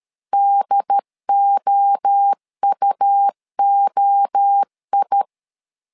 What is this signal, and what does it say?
Morse code, heard as a keyed tone, about 13 wpm, 800 Hz. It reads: DOUOI
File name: LEZIONI / 3 MWRO / 3 MWRO 12wpm 1730